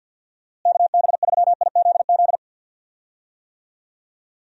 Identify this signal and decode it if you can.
Morse code, heard as a keyed tone, about 51 wpm, 700 Hz. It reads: KB3IZZ
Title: USA Call Signs - 6 Characters - Repeated 2x - 50wpm